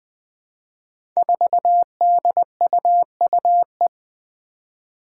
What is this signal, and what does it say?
Morse code, heard as a keyed tone, about 20 wpm, 700 Hz. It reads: 4DUUE
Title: Five Challenging Letter-Number Combinations 20wpm